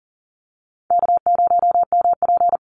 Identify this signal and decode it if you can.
Morse code, heard as a keyed tone, about 40 wpm, 700 Hz. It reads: K0MP
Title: Easy Call Signs - 40wpm